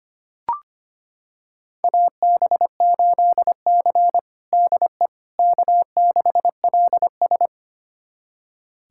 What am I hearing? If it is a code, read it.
Morse code, heard as a keyed tone, about 25 wpm, 700 Hz. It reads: AB8C DE K6LS
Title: USA Callsign DE Callsign - 4 Characters - 25wpm